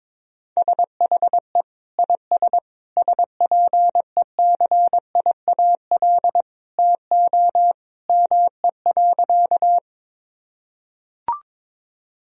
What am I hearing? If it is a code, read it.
Morse code, heard as a keyed tone, about 22 wpm, 690 Hz. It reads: SHE IS SPECIAL TO ME.
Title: Sentences from Top 1000 Words - Review Part 3 - 22wpm